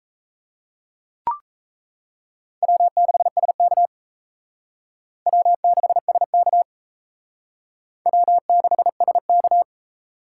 Morse code, heard as a keyed tone, about 35 wpm, 700 Hz. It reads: W6SK W6SK W6SK
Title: Speed Racing - USA 1x2 Call Signs 28wpm